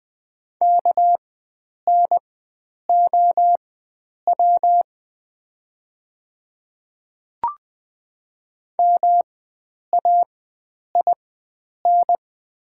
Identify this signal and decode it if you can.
Morse code, heard as a keyed tone, about 20 wpm, spaced as for 9 wpm, 700 Hz. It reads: KNOW MAIN
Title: 4x Character Spacing - Top 300 Words - 20wpm